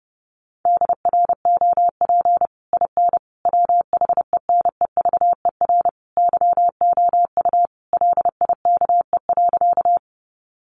Morse code, heard as a keyed tone, about 28 wpm, 700 Hz. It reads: DROP IN WHENEVER YOU LIKE.
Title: Sentences from Top 1400 Words 30wpm